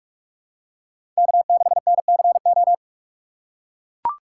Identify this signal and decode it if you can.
Morse code, heard as a keyed tone, about 42 wpm, 700 Hz. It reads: K6NCO
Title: USA Call Signs - 5 Characters - 6x Character Spacing - 45wpm